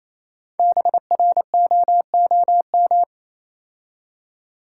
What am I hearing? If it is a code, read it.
Morse code, heard as a keyed tone, about 27 wpm, 690 Hz. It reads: BROOM
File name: Easy Words - Speed-Racing with Hints - 22wpm